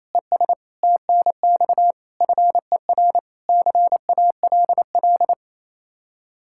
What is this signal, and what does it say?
Morse code, heard as a keyed tone, about 35 wpm, 700 Hz. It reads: ES TNX FER CALL
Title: Speed-Racing Common QSO Phrases - 28wpm